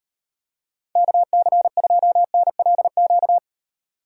Morse code, heard as a keyed tone, about 38 wpm, 700 Hz. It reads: KC2NLQ